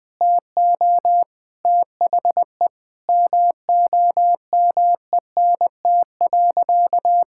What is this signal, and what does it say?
Morse code, heard as a keyed tone, about 21 wpm, 700 Hz. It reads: TO THE MOMENT.
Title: Sentences from Top 1000 Words - Review Part 5 - 20wpm